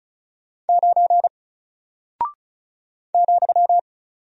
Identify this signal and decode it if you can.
Morse code, heard as a keyed tone, about 36 wpm, 700 Hz. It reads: 9 ,
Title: Single Character (Letter:Number:Punctuation:Prosign) Alt 35wpm